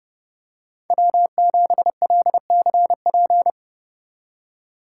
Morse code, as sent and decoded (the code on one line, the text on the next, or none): .-- --... .-.. -.-. .--.
W7LCP